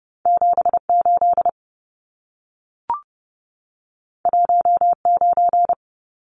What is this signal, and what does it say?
Morse code, heard as a keyed tone, about 30 wpm, 700 Hz. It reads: 78 19
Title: Numbers - 2-Digit 30wpm